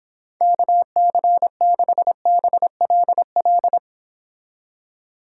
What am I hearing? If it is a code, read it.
Morse code, heard as a keyed tone, about 26 wpm, 700 Hz. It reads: KC6BLL